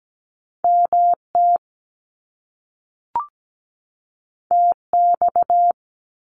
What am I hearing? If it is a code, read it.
Morse code, heard as a keyed tone, about 17 wpm, 700 Hz. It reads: MT TX
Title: US States 17wpm